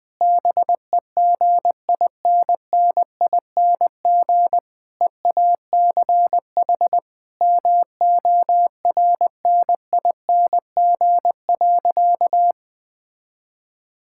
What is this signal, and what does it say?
Morse code, heard as a keyed tone, about 20 wpm, 700 Hz. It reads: BEGINNING EACH MORNING.
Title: Difficult Sentences with Hints - 20wpm